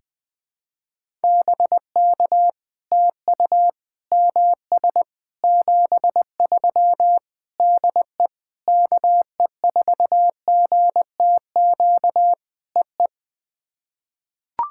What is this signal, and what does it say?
Morse code, heard as a keyed tone, about 20 wpm, 700 Hz. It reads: BK TU MS 73 DE KE4GTQ EE
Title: POTA QSO Practice - 20wpm